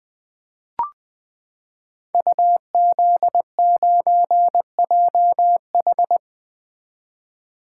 Morse code, heard as a keyed tone, about 20 wpm, 700 Hz. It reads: UZ9JH
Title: Code Groups - 5 Letter:Numbers - 20wpm